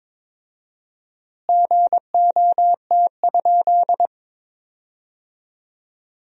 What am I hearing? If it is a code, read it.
Morse code, heard as a keyed tone, about 21 wpm, 700 Hz. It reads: GOT?
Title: 4x Word-Spacing - Sentences from Top 300 Words - 22wpm